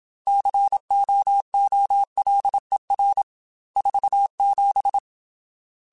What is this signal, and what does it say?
Morse code, heard as a keyed tone, about 27 wpm, 800 Hz. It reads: COOLER 47
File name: Weather Sets / Weather Sets-mp3s / Weather 5 5 @25